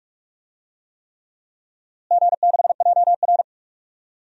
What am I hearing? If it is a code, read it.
Morse code, heard as a keyed tone, about 46 wpm, 700 Hz. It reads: G6JR
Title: DXCC-100 Call Signs - 4 Characters - Repeated 3x - 45wpm